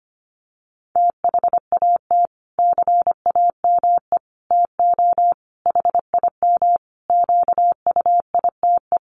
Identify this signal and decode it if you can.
Morse code, heard as a keyed tone, about 25 wpm, 700 Hz. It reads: THAT CAME TO HIM QUITE
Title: Sentences from Top 1300 Words 25wpm